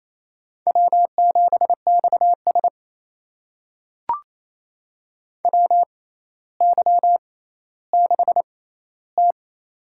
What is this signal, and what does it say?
Morse code, heard as a keyed tone, about 28 wpm, spaced as for 14 wpm, 700 Hz. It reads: W7XS WY6T
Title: USA Call Signs - 4 Characters - 6x Character Spacing - 28wpm